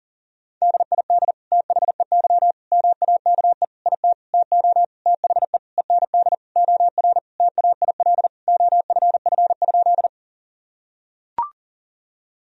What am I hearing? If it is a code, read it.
Morse code, heard as a keyed tone, about 40 wpm, 700 Hz. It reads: DID THEY MAKE IT TO THE END OR TAIL OFF?
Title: Sentences from Top 1000 Words - Review Part 3 - 40wpm